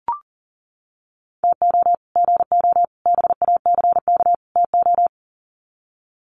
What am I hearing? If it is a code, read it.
Morse code, heard as a keyed tone, about 38 wpm, 700 Hz. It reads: TO GO BACK TO
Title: Sets of 4 Words 40wpm